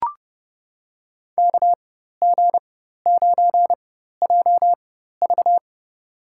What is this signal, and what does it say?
Morse code, heard as a keyed tone, about 29 wpm, spaced as for 14 wpm, 700 Hz. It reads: KG9JV
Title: USA Call Signs - 5 Characters - 4x Character Spacing - 30wpm